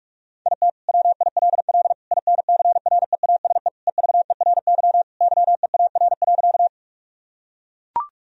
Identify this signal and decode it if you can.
Morse code, heard as a keyed tone, about 41 wpm, 700 Hz. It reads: IT WILL INCREASE EVERY YEAR.
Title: Sentences from Top 1000 Words - Review Part 4 - 45wpm